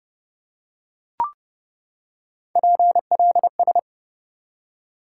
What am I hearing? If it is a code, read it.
Morse code, heard as a keyed tone, about 32 wpm, 700 Hz. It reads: PLS
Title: Top 90 QSO Elements 30wpm